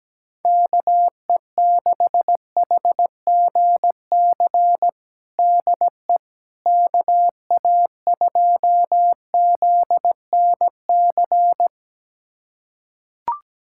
Morse code, heard as a keyed tone, about 17 wpm, 700 Hz. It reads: KE6HGC DE KA2ZNC
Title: USA Callsign DE Callsign - 6 Characters - 17wpm